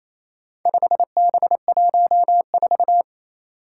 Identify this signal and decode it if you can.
Morse code, heard as a keyed tone, about 27 wpm, 700 Hz. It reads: 5B14
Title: Four Challenging Letter-Number Combinations 28wpm